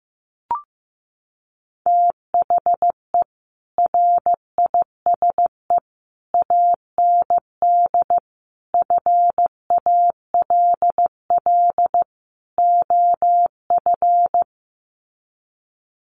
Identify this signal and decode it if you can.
Morse code, heard as a keyed tone, about 15 wpm, 700 Hz. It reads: THE RISE AND FALL OF
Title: Sets of 5 Words Encore2 15wpm